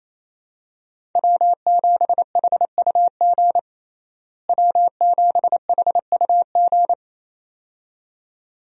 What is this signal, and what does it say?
Morse code, heard as a keyed tone, about 28 wpm, 700 Hz. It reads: W7HUG W7HUG
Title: USA Call Signs - 5 Characters - Repeated 2x - 28wpm